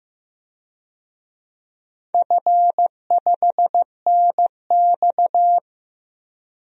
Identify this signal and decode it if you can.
Morse code, heard as a keyed tone, about 15 wpm, 700 Hz. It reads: F5NX
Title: DXCC-100 Call Signs - 4 Characters - Repeated 3x - 15wpm